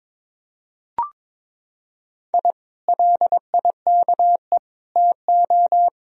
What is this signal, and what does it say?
Morse code, heard as a keyed tone, about 22 wpm, 700 Hz. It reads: I LIKE TO
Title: Sentences from Top 1000 Words - Review Part 3 - 22wpm